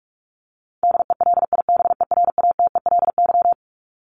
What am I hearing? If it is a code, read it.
Morse code, heard as a keyed tone, about 44 wpm, 700 Hz. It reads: DELIBERATELY
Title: Top 1751-2000 Words 45wpm